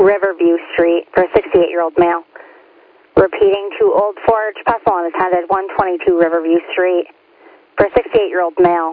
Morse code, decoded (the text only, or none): none